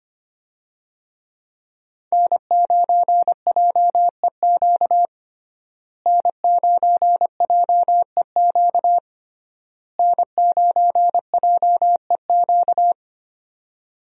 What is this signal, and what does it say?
Morse code, heard as a keyed tone, about 25 wpm, 700 Hz. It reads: N9JEQ N9JEQ N9JEQ
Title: USA Call Signs - 5 Characters - Repeated 3x - 25wpm